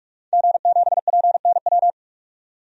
Morse code, heard as a keyed tone, about 46 wpm, 700 Hz. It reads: G7PNW
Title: DXCC-100 Call Signs - 5 Characters - 45wpm